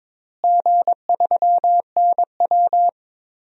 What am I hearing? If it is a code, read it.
Morse code, heard as a keyed tone, about 22 wpm, 700 Hz. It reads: G3NW
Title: DXCC-100 Call Signs - 4 Characters - Repeated 3x - 22wpm